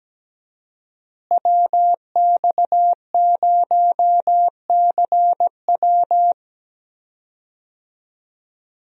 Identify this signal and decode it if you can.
Morse code, heard as a keyed tone, about 17 wpm, 700 Hz. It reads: WX0CW
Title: USA Call Signs - 5 Characters - Repeated 3x - 17wpm